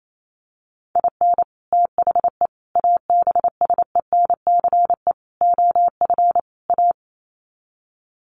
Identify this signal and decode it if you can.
Morse code, heard as a keyed tone, about 28 wpm, 700 Hz. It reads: IN THE ABSENCE OF A
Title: Sets of 5 Words Encore 28wpm